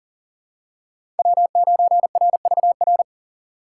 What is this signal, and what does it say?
Morse code, heard as a keyed tone, about 40 wpm, 700 Hz. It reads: W9RVR
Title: USA Call Signs - 5 Characters - 2x Character Spacing - 40wpm